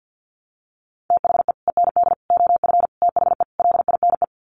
Morse code, heard as a keyed tone, about 50 wpm, 700 Hz. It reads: THE END OF THE LINE
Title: Sets of 5 Words Encore2 50wpm